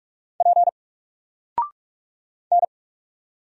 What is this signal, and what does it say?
Morse code, heard as a keyed tone, about 46 wpm, 700 Hz. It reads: P N
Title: Single Character (Letter:Number:Punctuation:Prosign) 45wpm